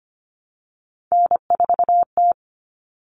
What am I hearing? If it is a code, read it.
Morse code, heard as a keyed tone, about 25 wpm, 700 Hz. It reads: N4T
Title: All Cut Numbers - 3-Digit 25wpm